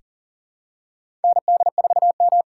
Morse code, heard as a keyed tone, about 40 wpm, 700 Hz. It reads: ND4M
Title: USA Call Signs - 2x Character Spacing - 40wpm